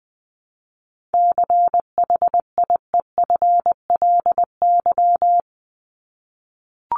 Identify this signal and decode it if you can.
Morse code, heard as a keyed tone, about 20 wpm, 700 Hz. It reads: CHIEFLY
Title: LY Suffix - Top 100 Words with Base Words 20wpm